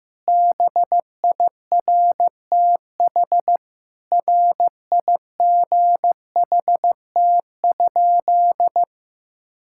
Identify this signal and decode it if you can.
Morse code, heard as a keyed tone, about 15 wpm, 700 Hz. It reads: BIRTH RIGHT?
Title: Sentences from Top 1000 Words - Review Part 5 - 15wpm